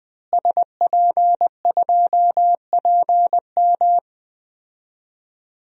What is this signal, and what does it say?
Morse code, heard as a keyed tone, about 20 wpm, 700 Hz. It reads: SP2PM
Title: DXCC-100 Call Signs - 5 Characters - Repeated 3x - 20wpm